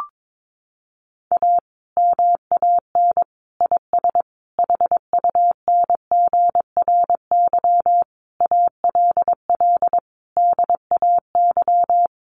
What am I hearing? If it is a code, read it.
Morse code, heard as a keyed tone, about 22 wpm, 700 Hz. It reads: A MAN IS HUNGRY ALL DAY
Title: Sentences from Top 1400 Words 22wpm